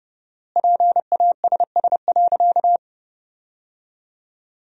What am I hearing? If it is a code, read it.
Morse code, heard as a keyed tone, about 30 wpm, 700 Hz. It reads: PASS.
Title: 4x Word-Spacing - Sentences from Top 400 Words - 30wpm